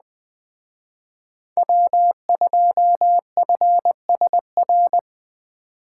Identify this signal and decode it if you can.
Morse code, heard as a keyed tone, about 20 wpm, 700 Hz. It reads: W2FSR
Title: Speed Racing - USA Call Signs 17wpm